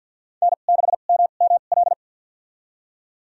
Morse code, heard as a keyed tone, about 52 wpm, spaced as for 36 wpm, 700 Hz. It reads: N6MML
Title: USA Call Signs - 5 Characters - 2x Character Spacing - 50wpm